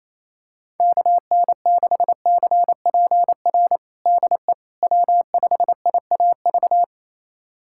Morse code, heard as a keyed tone, about 28 wpm, 700 Hz. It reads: KN6CPR DE W5IAV
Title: USA Callsign DE Callsign 28wpm